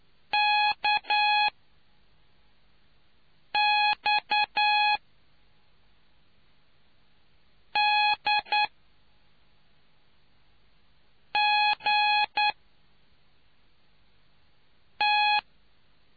Morse code, heard as a keyed tone, about 10 wpm, spaced as for 4 wpm, 840 Hz. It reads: KX D G T